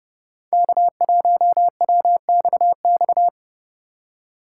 Morse code, heard as a keyed tone, about 30 wpm, 700 Hz. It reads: K1WXX
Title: USA Call Signs - 4x Character Spacing - 30wpm